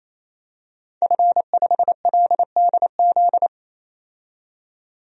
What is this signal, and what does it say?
Morse code, heard as a keyed tone, about 27 wpm, 700 Hz. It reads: F5LDZ
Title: DXCC-100 Call Signs - 5 Characters - 28wpm